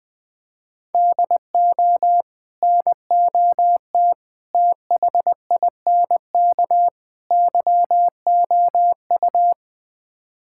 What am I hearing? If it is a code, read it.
Morse code, heard as a keyed tone, about 21 wpm, 700 Hz. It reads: DO NOT THINK YOU
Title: Build A Phrase - 4-Word Phrases - 20wpm